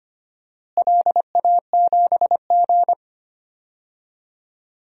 Morse code, heard as a keyed tone, about 25 wpm, 700 Hz. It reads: LA7G
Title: DXCC-100 Call Signs - 4 Characters - Repeated 2x - 25wpm